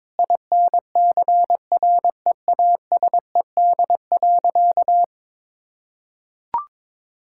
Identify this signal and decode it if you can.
Morse code, heard as a keyed tone, about 22 wpm, 700 Hz. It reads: INCREASED.
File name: practice / 22wpm / Sentences with 8-Word Phrases - 22wpm